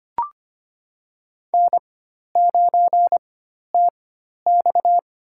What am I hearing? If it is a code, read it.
Morse code, heard as a keyed tone, about 25 wpm, spaced as for 12 wpm, 700 Hz. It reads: N9TX
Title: USA Call Signs - 4x Character Spacing - 25wpm